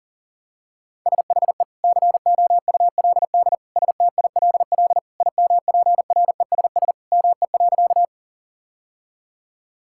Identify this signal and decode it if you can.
Morse code, heard as a keyed tone, about 40 wpm, 700 Hz. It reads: SHE COULD STILL IMPRESS ME.
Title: Sentences from Top 1600 Words 40wpm